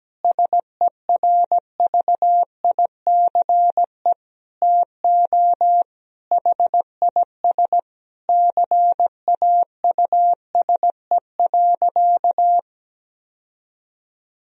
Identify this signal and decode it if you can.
Morse code, heard as a keyed tone, about 17 wpm, 700 Hz. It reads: SERVICE TO HIS CAUSE.
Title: Sentences from Top 1000 Words - Review Part 4 - 17wpm